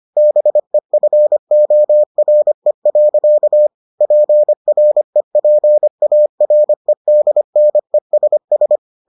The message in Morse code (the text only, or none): BEFORE. PREPAREDNESS